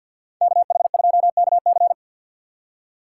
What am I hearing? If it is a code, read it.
Morse code, heard as a keyed tone, about 52 wpm, 700 Hz. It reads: KH2KC
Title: USA Call Signs - 5 Characters - 2x Character Spacing - 50wpm